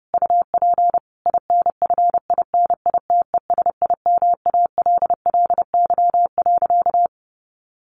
Morse code, heard as a keyed tone, about 30 wpm, 700 Hz. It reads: UP INFINITESIMALLY.
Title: A Sound Of Thunder 30wpm